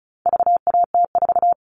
Morse code, heard as a keyed tone, about 34 wpm, 700 Hz. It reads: VAT4
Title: All Cut Numbers - 4-Digit 35wpm